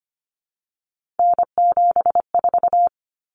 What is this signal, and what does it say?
Morse code, heard as a keyed tone, about 25 wpm, 700 Hz. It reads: N74
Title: Common Cut Numbers - 3-Digit 25wpm